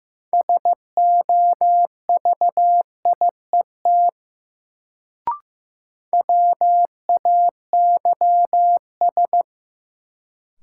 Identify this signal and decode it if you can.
Morse code, heard as a keyed tone, about 15 wpm, 700 Hz. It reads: SOVIET WAYS
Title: Sets of 5 Words Encore - New Word Drill - 15wpm